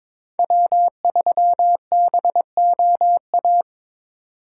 Morse code, heard as a keyed tone, about 24 wpm, 700 Hz. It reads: W3BOA
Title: Speed Racing - USA 1x3 Call Signs 22wpm